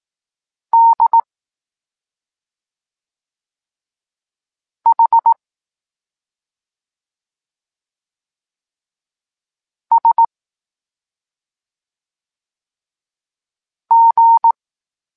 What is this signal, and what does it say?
Morse code, heard as a keyed tone, about 18 wpm, 920 Hz. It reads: D H S G
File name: Tekst-23 4